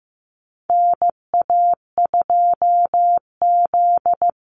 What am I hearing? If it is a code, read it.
Morse code, heard as a keyed tone, about 15 wpm, 700 Hz. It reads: NA2Z
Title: USA Call Signs 15wpm